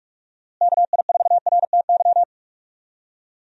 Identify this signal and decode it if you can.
Morse code, heard as a keyed tone, about 43 wpm, 700 Hz. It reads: KI4RTY
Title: USA Call Signs - 6 Characters - Repeated 2x - 45wpm